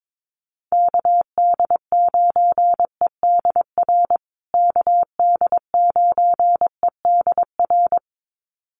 Morse code, heard as a keyed tone, about 22 wpm, 700 Hz. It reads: KD9EDR KD9EDR
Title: USA Call Signs Repeated 2x 22wpm